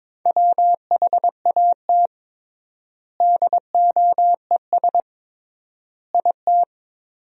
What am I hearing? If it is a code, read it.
Morse code, heard as a keyed tone, about 22 wpm, 700 Hz. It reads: WHAT DOES IT
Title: 3x Word-Spacing - Sentences from Top 200 Words - 22wpm